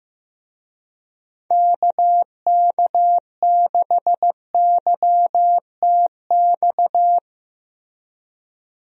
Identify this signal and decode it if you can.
Morse code, heard as a keyed tone, about 15 wpm, 700 Hz. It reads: KK6YTX
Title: USA Call Signs - 6 Characters - Repeated 2x - 15wpm